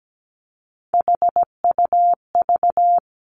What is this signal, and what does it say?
Morse code, heard as a keyed tone, about 17 wpm, 700 Hz. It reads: HUV
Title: Three Challenging Letter-Number Combinations 17wpm